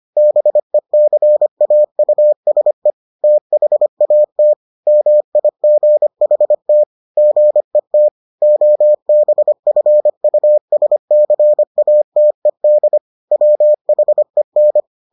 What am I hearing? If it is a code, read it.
Morse code, heard as a keyed tone, about 25 wpm, 600 Hz. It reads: BECAUSE THAT MIGHT GET OBFUSCATED WHEN